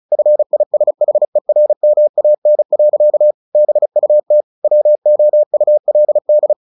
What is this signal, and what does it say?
Morse code, heard as a keyed tone, about 34 wpm, 600 Hz. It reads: FISHERMAN. BUT WOULD
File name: the old man and the sea / the old man and the sea 1 35wpm